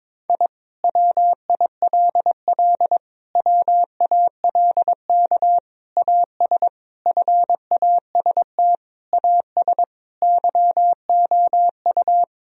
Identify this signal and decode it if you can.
Morse code, heard as a keyed tone, about 21 wpm, 700 Hz. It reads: I WILL WALK AS FAST AS YOU
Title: Sentences from Top 1000 Words - Review Part 2 - 22wpm